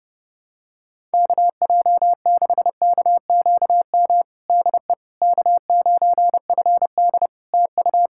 Morse code, heard as a keyed tone, about 30 wpm, 700 Hz. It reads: KJ6KQM DE K9FD TU
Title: K1USN SST QSO Practice 30wpm